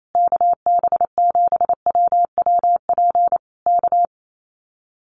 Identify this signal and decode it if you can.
Morse code, heard as a keyed tone, about 28 wpm, 700 Hz. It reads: KB7WWP K
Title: Common QSO Phrases Encore 28wpm